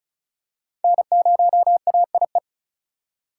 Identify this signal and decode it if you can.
Morse code, heard as a keyed tone, about 36 wpm, 700 Hz. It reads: N0AIE
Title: K1USN SST QSO Practice - 35wpm